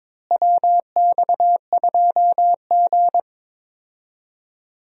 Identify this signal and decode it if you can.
Morse code, heard as a keyed tone, about 22 wpm, 700 Hz. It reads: WX2G
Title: USA Call Signs - 4 Characters - Repeated 2x - 22wpm